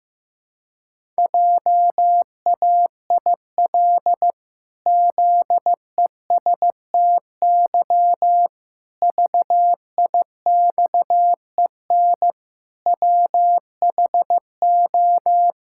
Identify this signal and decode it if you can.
Morse code, heard as a keyed tone, about 15 wpm, 700 Hz. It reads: JAIL ZESTY VIXEN WHO
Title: Pangram Challenge v2 - 15wpm